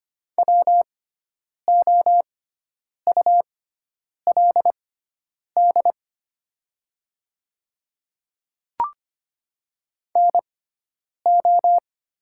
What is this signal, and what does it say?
Morse code, heard as a keyed tone, about 25 wpm, spaced as for 9 wpm, 700 Hz. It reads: WOULD NO
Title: 6x Character Spacing - Top 100 Words - 25wpm